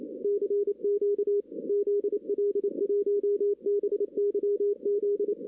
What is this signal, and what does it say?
Morse code, heard as a keyed tone, about 28 wpm, 410 Hz. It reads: CQ ZL1BYZ
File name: 2016 arrl dx cw / 2016-ARRL-DX-CW-K5TR- - 0-9